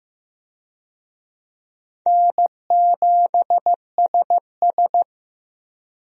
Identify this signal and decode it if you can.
Morse code, heard as a keyed tone, about 15 wpm, 700 Hz. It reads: N7SS
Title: USA Call Signs - 4 Characters - Repeated 2x - 15wpm